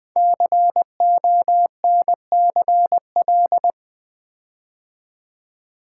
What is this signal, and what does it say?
Morse code, heard as a keyed tone, about 20 wpm, 700 Hz. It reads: CONCL
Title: Build A Word - 5 Letters At A Time - 10-Letter Words - 20wpm